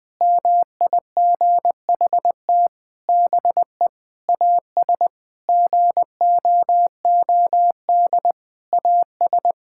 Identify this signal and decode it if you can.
Morse code, heard as a keyed tone, about 21 wpm, 700 Hz. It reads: MIGHT BE AS GOOD AS
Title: Sentences from Top 1600 Words 20wpm